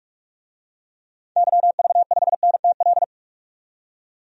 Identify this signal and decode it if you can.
Morse code, heard as a keyed tone, about 44 wpm, 700 Hz. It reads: YV5NTL
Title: DXCC-100 Call Signs - 6 Characters - 45wpm